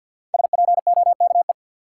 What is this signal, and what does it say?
Morse code, heard as a keyed tone, about 52 wpm, 700 Hz. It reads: SPOKE
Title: Sentences from Top 1200 Words - New Word Drill - 50wpm